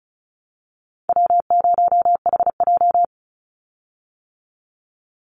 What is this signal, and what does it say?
Morse code, heard as a keyed tone, about 34 wpm, 700 Hz. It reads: W0HJ
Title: USA Call Signs 35wpm